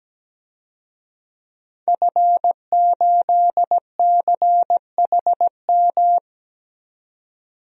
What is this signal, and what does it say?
Morse code, heard as a keyed tone, about 17 wpm, 700 Hz. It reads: F8CHM
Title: DXCC-100 Call Signs - 5 Characters - 17wpm